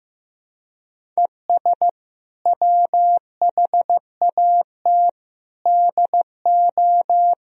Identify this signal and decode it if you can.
Morse code, heard as a keyed tone, about 16 wpm, 700 Hz. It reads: ES WHAT DO
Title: Sentences from Top 100 Words and Abbreviations 15wpm